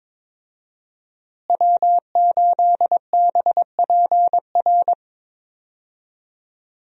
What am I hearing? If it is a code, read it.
Morse code, heard as a keyed tone, about 22 wpm, 700 Hz. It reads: W8BPR
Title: USA Call Signs - 5 Characters - Repeated 3x - 22wpm